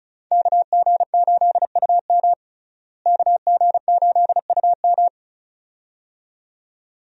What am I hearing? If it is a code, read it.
Morse code, heard as a keyed tone, about 36 wpm, 700 Hz. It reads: KG8UM KG8UM
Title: USA Call Signs - 5 Characters - Repeated 2x - 35wpm